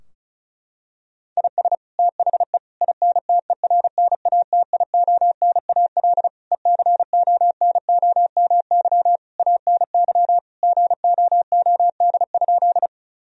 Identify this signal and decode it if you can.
Morse code, heard as a keyed tone, about 36 wpm, 700 Hz. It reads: IS THE INTERNATIONAL ECONOMY ANY GOOD?
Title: Sentences from Top 1000 Words - Review Part 4 - 35wpm